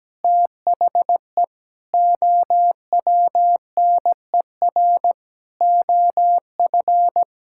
Morse code, heard as a keyed tone, about 17 wpm, 700 Hz. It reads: THE OWNER OF
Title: Sentences from Top 1600 Words 17wpm